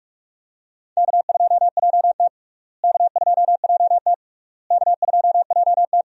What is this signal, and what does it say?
Morse code, heard as a keyed tone, about 47 wpm, 700 Hz. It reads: K2JT K2JT K2JT
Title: USA Call Signs - 4 Characters - Repeated 3x - 45wpm